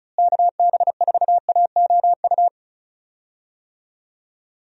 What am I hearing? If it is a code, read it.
Morse code, heard as a keyed tone, about 36 wpm, 700 Hz. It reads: KB4AOU
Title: USA Call Signs - 6 Characters - Repeated 3x - 35wpm